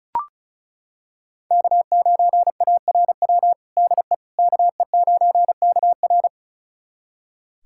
Morse code, heard as a keyed tone, about 36 wpm, 700 Hz. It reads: K9ARW DE KE9KR
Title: USA Callsign DE Callsign - 5 Characters - 35wpm